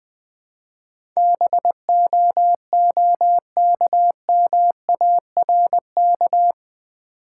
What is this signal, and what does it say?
Morse code, heard as a keyed tone, about 20 wpm, 700 Hz. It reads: BOOKMARK
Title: Easy Words with Hints - 20wpm